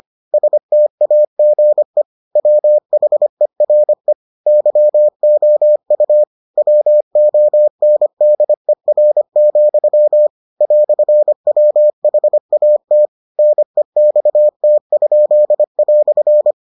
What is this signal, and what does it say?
Morse code, heard as a keyed tone, about 25 wpm, 600 Hz. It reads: STAGE WHERE YOU WONDER, "WHAT NEXT?"